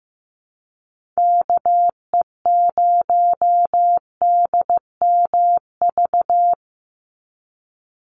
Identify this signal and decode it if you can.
Morse code, heard as a keyed tone, about 15 wpm, 700 Hz. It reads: KE0DMV
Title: USA Call Signs 15wpm